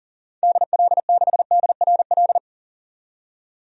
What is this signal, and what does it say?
Morse code, heard as a keyed tone, about 37 wpm, 700 Hz. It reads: DL6DRL
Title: DXCC-100 Call Signs - 6 Characters - 40wpm